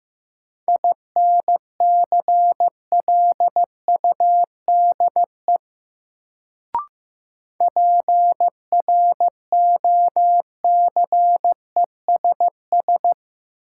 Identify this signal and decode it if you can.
Morse code, heard as a keyed tone, about 15 wpm, 700 Hz. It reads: INCLUDE PROCESS
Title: Sentences from Top 1000 Words - Review Part 3 - New Word Drill - 15wpm